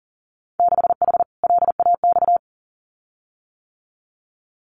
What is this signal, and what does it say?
Morse code, heard as a keyed tone, about 38 wpm, 700 Hz. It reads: 6H LAX
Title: Winter Field Day Exchanges Encore 40wpm